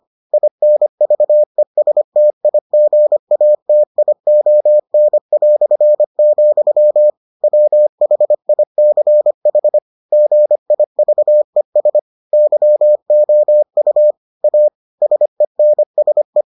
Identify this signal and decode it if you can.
Morse code, heard as a keyed tone, about 25 wpm, 600 Hz. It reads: INVESTIGATION", WHICH GIVES YOU A SENSE